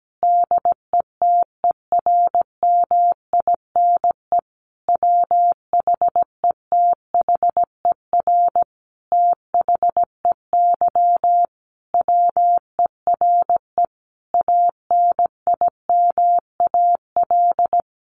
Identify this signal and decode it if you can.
Morse code, heard as a keyed tone, about 17 wpm, 700 Hz. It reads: DETERMINE WHETHER THEY WERE ANIMAL
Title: Sentences from Top 1300 Words 17wpm